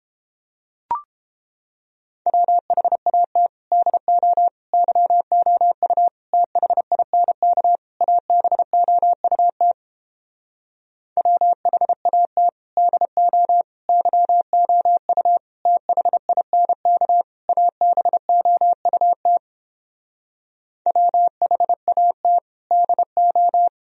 Morse code, heard as a keyed tone, about 29 wpm, 700 Hz. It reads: WHAT DO YOU THINK ABOUT WHAT DO YOU THINK ABOUT WHAT DO